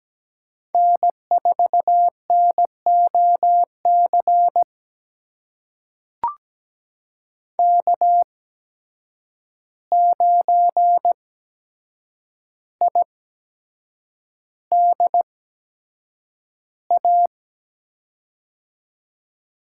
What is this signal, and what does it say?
Morse code, heard as a keyed tone, about 17 wpm, spaced as for 7 wpm, 700 Hz. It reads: N4NOC K9IDA
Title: USA Call Signs - 5 Characters - 8x Character Spacing - 17wpm